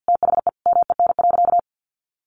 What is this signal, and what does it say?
Morse code, heard as a keyed tone, about 50 wpm, 700 Hz. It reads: THE MEN.
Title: A Sound Of Thunder 50wpm